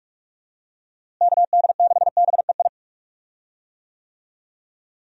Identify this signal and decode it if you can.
Morse code, heard as a keyed tone, about 46 wpm, 700 Hz. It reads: KD6BEI
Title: USA Call Signs - 6 Characters - Repeated 3x - 45wpm